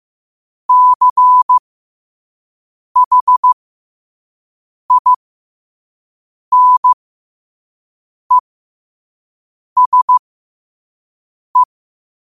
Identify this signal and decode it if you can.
Morse code, heard as a keyed tone, about 15 wpm, spaced as for 5 wpm, 1000 Hz. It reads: CHINESE
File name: qotdfiles / 5wpm